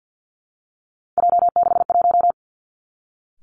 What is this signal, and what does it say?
Morse code, heard as a keyed tone, about 50 wpm, 700 Hz. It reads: J61
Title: Three Challenging Letter-Number Combinations 50wpm